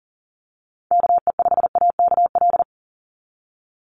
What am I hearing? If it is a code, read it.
Morse code, heard as a keyed tone, about 42 wpm, 700 Hz. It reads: KE5AKL
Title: NA SOTA Activator Callsigns 40wpm